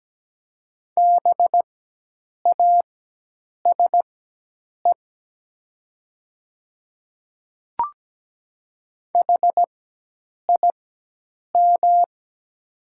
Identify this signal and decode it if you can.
Morse code, heard as a keyed tone, about 17 wpm, spaced as for 8 wpm, 700 Hz. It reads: BASE HIM
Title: 4x Character Spacing - Top 300 Words - 17wpm